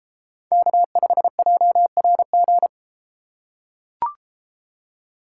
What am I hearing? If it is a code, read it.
Morse code, heard as a keyed tone, about 32 wpm, 700 Hz. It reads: K5JRG